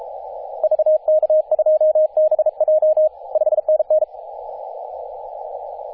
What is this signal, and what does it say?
Morse code, heard as a keyed tone, about 33 wpm, 620 Hz, with radio noise. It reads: VK2BJ 5NN
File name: t30cow-18-072mhz-cw